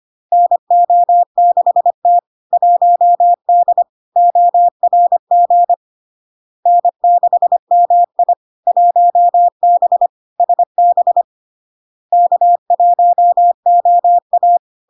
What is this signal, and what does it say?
Morse code, heard as a keyed tone, about 25 wpm, 700 Hz. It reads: NO6T 1D ORG N6MI 1B SB K1OA
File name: Log-in-CW-25wpm-700-Hz